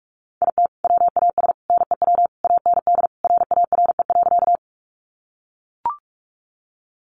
Morse code, heard as a keyed tone, about 44 wpm, 700 Hz. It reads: IT WAS NEW AND RARE.
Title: Sentences from Top 1100 Words 45wpm